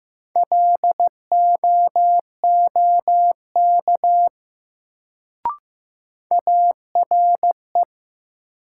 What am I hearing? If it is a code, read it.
Morse code, heard as a keyed tone, about 15 wpm, 700 Hz. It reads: LOOK ARE
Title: Sentences from Top 100 Words - New Word Drill - 15wpm